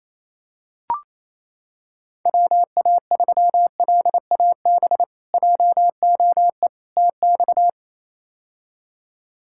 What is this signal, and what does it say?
Morse code, heard as a keyed tone, about 28 wpm, 700 Hz. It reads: WA3LAB JOE TX
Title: K1USN SST QSO Practice - 28wpm